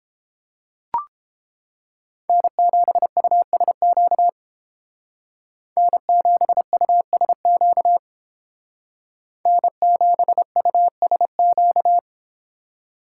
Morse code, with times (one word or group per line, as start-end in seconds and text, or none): 2.3-4.3 N7USQ
5.8-8.0 N7USQ
9.5-12.0 N7USQ